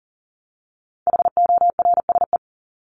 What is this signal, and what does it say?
Morse code, heard as a keyed tone, about 40 wpm, 700 Hz. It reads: HORSE
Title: Top 300 Words 40wpm